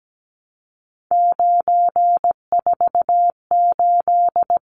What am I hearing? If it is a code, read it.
Morse code, heard as a keyed tone, about 17 wpm, 700 Hz. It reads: 948
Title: Numbers - 3-Digit 17wpm